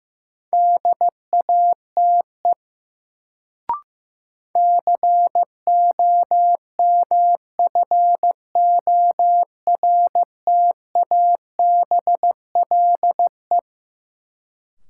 Morse code, heard as a keyed tone, about 15 wpm, 700 Hz. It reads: DATE COMFORTABLE
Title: Sentences from Top 1000 Words - Review Part 7 - New Word Drill - 15wpm